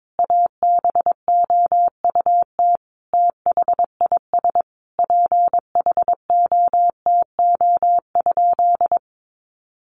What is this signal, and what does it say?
Morse code, heard as a keyed tone, about 22 wpm, 700 Hz. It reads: ABOUT THIS PHOTO?
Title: Sentences from Top 1200 Words 22wpm